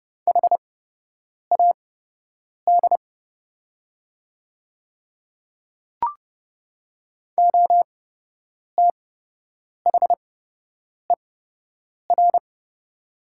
Morse code, heard as a keyed tone, about 30 wpm, spaced as for 8 wpm, 700 Hz. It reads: HAD OTHER
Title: 8x Character Spacing - Top 200 Words - 30wpm